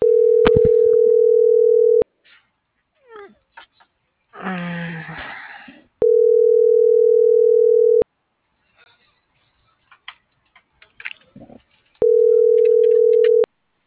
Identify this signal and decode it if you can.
Ringback tone (North America), 440 + 480 Hz.